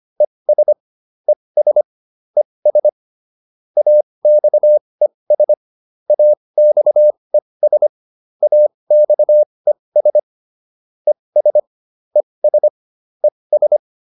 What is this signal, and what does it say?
Morse code, heard as a keyed tone, about 25 wpm, spaced as for 20 wpm, 600 Hz. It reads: ES ES ES AXES AXES AXES ES ES ES
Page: advanced-practice-files